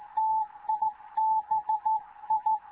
Morse code, heard as a keyed tone, about 16 wpm, 870 Hz. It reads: TIBI
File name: Morze-HA2MN / hangp / 2 11 GYAKORLAT